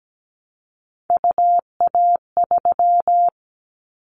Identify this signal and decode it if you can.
Morse code, heard as a keyed tone, about 17 wpm, 700 Hz. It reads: UA3
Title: Callsign Prefixes for Top 25 DXCC - 17wpm